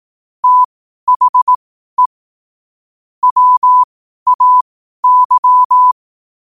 Morse code, heard as a keyed tone, about 18 wpm, spaced as for 12 wpm, 1000 Hz. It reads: THE WAY
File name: qotdfiles / 12wpm